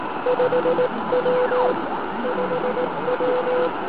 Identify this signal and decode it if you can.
Morse code, heard as a keyed tone, about 18 wpm, 500 Hz, with noise. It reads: 5W 5W